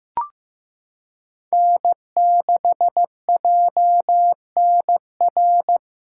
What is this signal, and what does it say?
Morse code, heard as a keyed tone, about 15 wpm, 700 Hz. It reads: N6JNR
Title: USA Callsign DE Callsign 15wpm